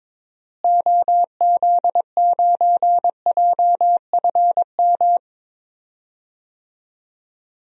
Morse code, heard as a keyed tone, about 22 wpm, 700 Hz. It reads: OZ9JFM
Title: DXCC-100 Call Signs - 6 Characters - Repeated 2x - 22wpm